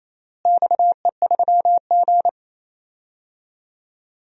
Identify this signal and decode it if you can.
Morse code, heard as a keyed tone, about 27 wpm, 700 Hz. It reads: XE3G
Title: DXCC-100 Call Signs - 4 Characters - Repeated 3x - 28wpm